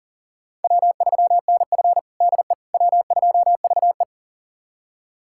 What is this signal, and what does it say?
Morse code, heard as a keyed tone, about 40 wpm, 700 Hz. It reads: W3NF DE W2VE
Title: USA Callsign DE Callsign - 4 Characters - 40wpm